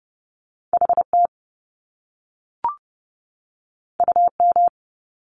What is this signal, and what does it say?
Morse code, heard as a keyed tone, about 30 wpm, 700 Hz. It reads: HT UM
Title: Top Two Letter Combinations in Words 30wpm